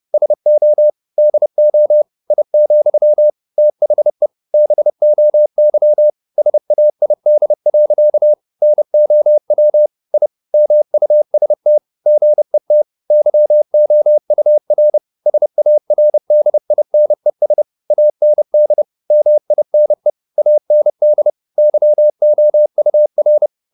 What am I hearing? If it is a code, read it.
Morse code, heard as a keyed tone, about 30 wpm, 600 Hz. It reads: SO DO I, THE BOY SAID. NOW I MUST GET YOUR SARDINES AND MINE AND YOUR
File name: the old man and the sea / the old man and the sea 1 30wpm